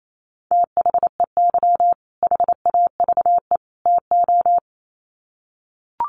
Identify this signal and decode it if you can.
Morse code, heard as a keyed tone, about 28 wpm, 700 Hz. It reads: THEY HAVE TO
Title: Sets of 3 Words 28wpm